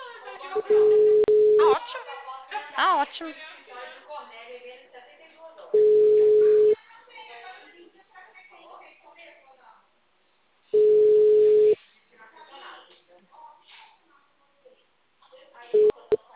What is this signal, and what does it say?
Ringback tone (Europe, 425 Hz), with speech.
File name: out- -2004- - - .42890